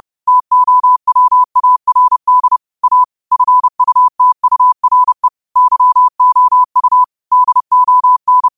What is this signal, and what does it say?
Morse code, heard as a keyed tone, about 31 wpm, 1000 Hz. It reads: TOWARD A FUTURE YOU DON
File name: qotdfiles / 30wpm